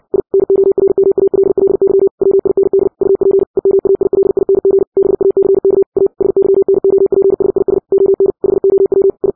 Morse code, heard as a keyed tone, about 60 wpm, 390 Hz. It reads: I NORMALLY WIND UP EMAILING BACK A SYNOPSIS ON HOW I
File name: 60wpm